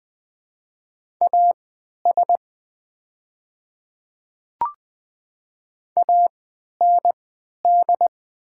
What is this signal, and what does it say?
Morse code, heard as a keyed tone, about 20 wpm, spaced as for 11 wpm, 700 Hz. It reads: AS AND
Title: 3x Character Spacing - Top 100 Words - 20wpm